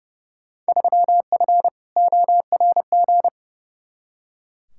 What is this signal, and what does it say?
Morse code, heard as a keyed tone, about 30 wpm, 700 Hz. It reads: 3F ORG
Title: Field Day Exchanges - 30wpm